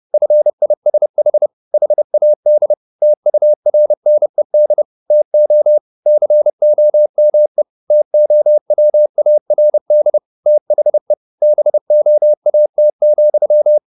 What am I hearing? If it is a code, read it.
Morse code, heard as a keyed tone, about 30 wpm, 600 Hz. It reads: FISH HAD TURNED TO COME TOWARD THE BOAT,